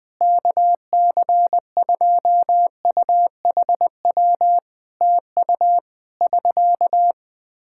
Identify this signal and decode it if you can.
Morse code, heard as a keyed tone, about 20 wpm, 700 Hz. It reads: KC2UHW TU <SK>
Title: Practice QSO - Exchange 4 - Ending - 20wpm